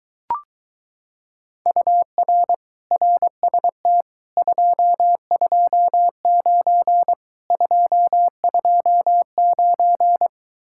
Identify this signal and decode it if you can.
Morse code, heard as a keyed tone, about 23 wpm, 700 Hz. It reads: UR RST 229 229